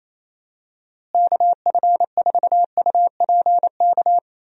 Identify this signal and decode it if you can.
Morse code, heard as a keyed tone, about 28 wpm, 700 Hz. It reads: KF4UPK
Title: USA Call Signs - 6 Characters - Repeated 3x - 28wpm